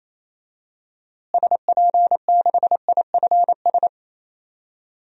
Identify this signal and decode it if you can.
Morse code, heard as a keyed tone, about 28 wpm, 700 Hz. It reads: SP6IFS
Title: DXCC-100 Call Signs - 6 Characters - Repeated 2x - 28wpm